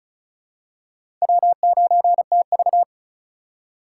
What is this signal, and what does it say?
Morse code, heard as a keyed tone, about 34 wpm, 700 Hz. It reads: W9TV
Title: USA Call Signs - 4 Characters - 4x Character Spacing - 35wpm